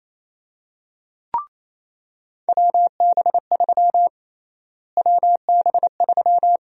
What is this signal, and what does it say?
Morse code, heard as a keyed tone, about 28 wpm, 700 Hz. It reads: WB3 WB3
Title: USA Call Sign Prefixes - Repeated 2x - 28wpm